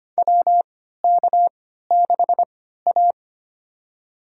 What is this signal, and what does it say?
Morse code, heard as a keyed tone, about 25 wpm, spaced as for 14 wpm, 700 Hz. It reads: WK6A
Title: USA Call Signs - 4 Characters - 3x Character Spacing - 25wpm